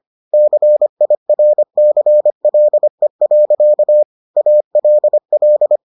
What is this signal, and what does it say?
Morse code, heard as a keyed tone, about 25 wpm, 600 Hz. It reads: CIRCLE. ALL